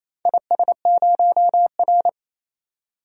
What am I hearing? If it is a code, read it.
Morse code, heard as a keyed tone, about 27 wpm, 690 Hz. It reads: IS0R
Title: DXCC-100 Call Signs - 4 Characters - 28wpm